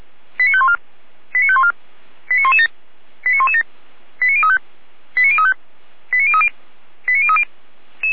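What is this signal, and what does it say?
Five-tone selective call (ZVEI), heard as quick runs of five tones, tones of about 70 ms. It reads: 89514 89514 89117 89117 89035 89035 89030 89030
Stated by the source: ZVEI five-tone selective call, received on an SDR.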